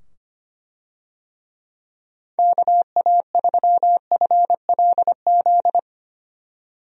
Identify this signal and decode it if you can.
Morse code, heard as a keyed tone, about 25 wpm, 700 Hz. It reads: KA3FLZ
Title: USA Call Signs - 6 Characters - 25wpm